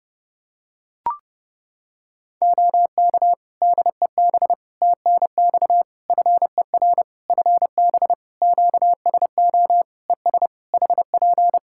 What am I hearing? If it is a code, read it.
Morse code, heard as a keyed tone, about 30 wpm, 700 Hz. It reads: OK DEB TNX FER FB QSO ES HP
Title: Practice QSO - Exchange 4 - Ending - 30wpm